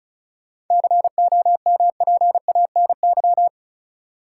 Morse code, heard as a keyed tone, about 36 wpm, 700 Hz. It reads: COMPANY
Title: Sets of 2 Words - New Word Drill - 35wpm